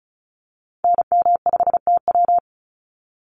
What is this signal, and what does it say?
Morse code, heard as a keyed tone, about 34 wpm, 700 Hz. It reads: NM5TW
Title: NA SOTA Activator Callsigns 35wpm